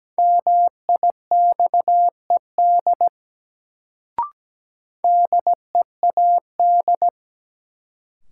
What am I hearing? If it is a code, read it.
Morse code, heard as a keyed tone, about 17 wpm, 700 Hz. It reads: MIXED DEAD
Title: Sentences from Top 1500 Words - New Word Drill - 17wpm